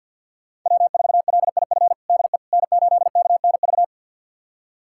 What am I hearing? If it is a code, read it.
Morse code, heard as a keyed tone, about 52 wpm, 700 Hz. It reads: W4LIF DE N8KNV
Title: USA Callsign DE Callsign - 5 Characters - 50wpm